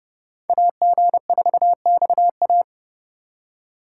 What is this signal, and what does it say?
Morse code, heard as a keyed tone, about 29 wpm, 700 Hz. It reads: AG4XA